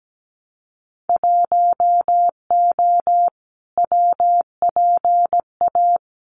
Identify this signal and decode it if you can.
Morse code, heard as a keyed tone, about 17 wpm, 700 Hz. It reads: 1O WPA
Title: Winter Field Day Exchanges 17wpm